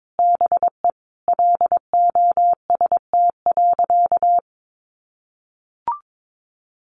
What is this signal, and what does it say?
Morse code, heard as a keyed tone, about 22 wpm, 700 Hz. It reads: BE LOST.
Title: Sentences from Top 1400 Words 22wpm